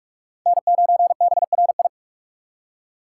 Morse code, heard as a keyed tone, about 59 wpm, 700 Hz. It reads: N9BRI